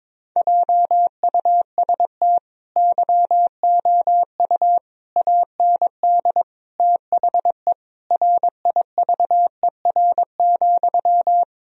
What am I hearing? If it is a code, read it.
Morse code, heard as a keyed tone, about 22 wpm, 700 Hz. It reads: JUST YOU AND THE RIVER,
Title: Sentences from Top 1000 Words - Review Part 1 - 22wpm